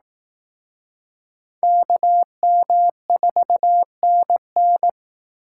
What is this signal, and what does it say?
Morse code, heard as a keyed tone, about 18 wpm, 700 Hz. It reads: KM4NN
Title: Speed Racing - USA 2x2 Call Signs 15wpm